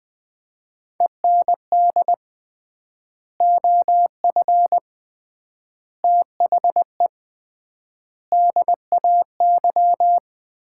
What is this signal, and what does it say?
Morse code, heard as a keyed tone, about 20 wpm, 700 Hz. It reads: END OF THE DAY
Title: 3x Word-Spacing - Sets of 4 Words - 20wpm